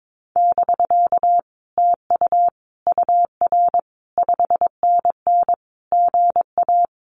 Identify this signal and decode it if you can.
Morse code, heard as a keyed tone, about 22 wpm, 700 Hz. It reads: <BK> TU UR 5NN GA
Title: Common QSO Phrases Encore 22wpm